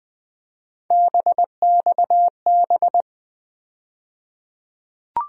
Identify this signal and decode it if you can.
Morse code, heard as a keyed tone, about 20 wpm, 700 Hz. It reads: BXB
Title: USA Call Sign Postfixes - 3 Letters 20wpm